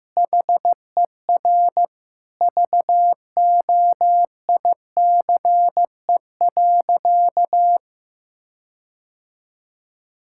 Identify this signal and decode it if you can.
Morse code, heard as a keyed tone, about 15 wpm, 700 Hz. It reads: HER VOICE.
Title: Sentences from Top 1600 Words 15wpm